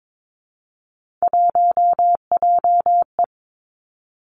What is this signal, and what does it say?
Morse code, heard as a keyed tone, about 22 wpm, 700 Hz. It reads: 1JE
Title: Three Challenging Letter-Number Combinations 22wpm